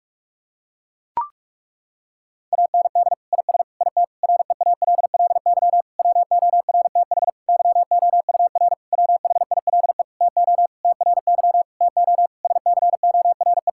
Morse code, heard as a keyed tone, about 41 wpm, 700 Hz. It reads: AND IS IT REALLY WORTH YOUR WHILE TO TRY TO SCORE
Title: Sentences from Top 1600 Words 45wpm